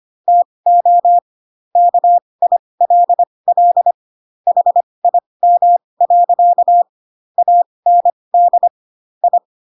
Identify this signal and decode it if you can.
Morse code, heard as a keyed tone, about 25 wpm, spaced as for 20 wpm, 700 Hz. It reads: TO KILL HIM. AND I
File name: Pinocchio / Pinocchio ch7 15 25 ik1hge